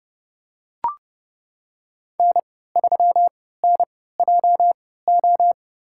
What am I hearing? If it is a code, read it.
Morse code, heard as a keyed tone, about 31 wpm, spaced as for 17 wpm, 700 Hz. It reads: N3NJO
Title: USA Call Signs - 5 Characters - 3x Character Spacing - 30wpm